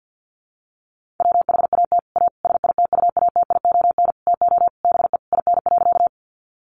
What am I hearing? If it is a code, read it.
Morse code, heard as a keyed tone, about 50 wpm, 700 Hz. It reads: WHAT A SITUATION TO BE IN.
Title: Sentences from Top 1200 Words 50wpm